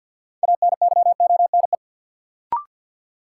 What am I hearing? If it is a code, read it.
Morse code, heard as a keyed tone, about 46 wpm, 700 Hz. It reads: ANYONE
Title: Sets of 5 Words Encore - New Word Drill - 50wpm